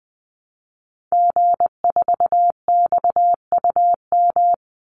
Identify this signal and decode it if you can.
Morse code, heard as a keyed tone, about 20 wpm, 700 Hz. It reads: G4XUM
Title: Call Signs - 20wpm